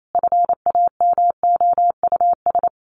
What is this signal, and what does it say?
Morse code, heard as a keyed tone, about 28 wpm, 700 Hz. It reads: FAMOUS
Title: Sets of 5 Words Encore2 28wpm